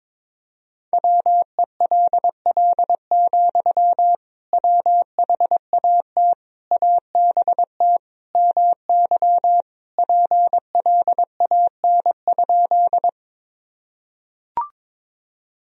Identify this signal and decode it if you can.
Morse code, heard as a keyed tone, about 22 wpm, 700 Hz. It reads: WELL, WHAT ABT MY PLAN?
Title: Sentences from Top 500 Words and Abbreviations 22wpm